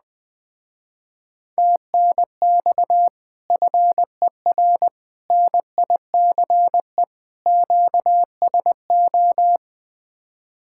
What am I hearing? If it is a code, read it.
Morse code, heard as a keyed tone, about 20 wpm, 700 Hz. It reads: TNX FER NICE QSO
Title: Speed-Racing Common QSO Phrases Encore - 17wpm